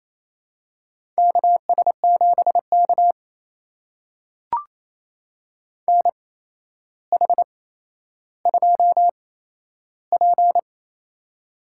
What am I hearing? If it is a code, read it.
Morse code, heard as a keyed tone, about 28 wpm, spaced as for 12 wpm, 700 Hz. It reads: KS7K NH2P
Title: USA Call Signs - 8x Character Spacing - 28wpm